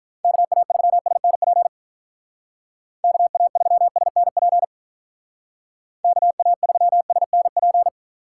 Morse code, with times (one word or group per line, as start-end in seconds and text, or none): 0.2-1.7 KA3SNP
3.0-4.6 KA3SNP
6.0-7.9 KA3SNP